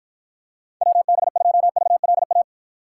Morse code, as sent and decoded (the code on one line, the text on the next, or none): .-- -... ..--- ...- .-.. .-
WB2VLA